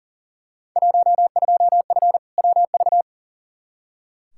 Morse code, heard as a keyed tone, about 37 wpm, 700 Hz. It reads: 12F WV
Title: Field Day Exchanges Encore - 40wpm